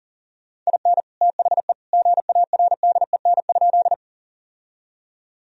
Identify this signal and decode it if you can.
Morse code, heard as a keyed tone, about 40 wpm, 700 Hz. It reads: IN THE GARDEN?
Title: Sentences from Top 1000 Words - Review Part 4 - 40wpm